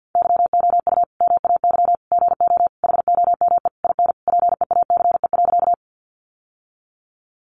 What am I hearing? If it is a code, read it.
Morse code, heard as a keyed tone, about 50 wpm, 700 Hz. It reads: YOU MAY GO HOME IN PEACE.
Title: Sentences from Top 1500 Words 50wpm